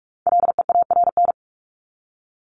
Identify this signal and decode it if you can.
Morse code, heard as a keyed tone, about 44 wpm, 700 Hz. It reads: LEARN
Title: ING Suffix - Top 100 Words with Base Words 45wpm